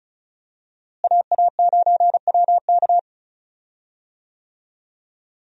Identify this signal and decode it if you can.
Morse code, heard as a keyed tone, about 36 wpm, 700 Hz. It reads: AA9WK
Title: USA Call Signs - 5 Characters - Repeated 3x - 35wpm